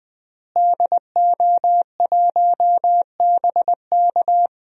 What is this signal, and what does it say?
Morse code, heard as a keyed tone, about 20 wpm, 700 Hz. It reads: DO1BK
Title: DXCC-100 Call Signs - 5 Characters - Repeated 3x - 20wpm